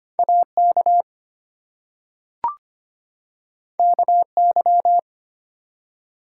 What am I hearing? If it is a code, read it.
Morse code, heard as a keyed tone, about 25 wpm, 700 Hz. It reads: AK KY
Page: morse-code-ninja-practice-25wpm